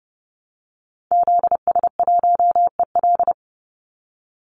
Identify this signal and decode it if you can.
Morse code, heard as a keyed tone, about 30 wpm, 700 Hz. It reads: ZS1EL
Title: Call Signs Repeated 2x - 30wpm